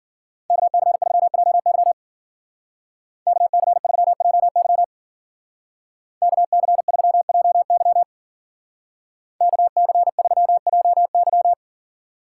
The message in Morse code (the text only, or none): KC3JY KC3JY KC3JY KC3JY